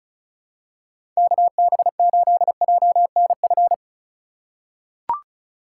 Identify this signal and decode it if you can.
Morse code, heard as a keyed tone, about 36 wpm, 700 Hz. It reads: KB8JNF
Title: USA Call Signs - 2x Character Spacing - 35wpm